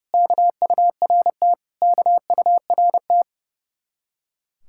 Morse code, heard as a keyed tone, about 37 wpm, 700 Hz. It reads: KURT KURT
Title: Speed-Racing Common QSO Phrases - 30wpm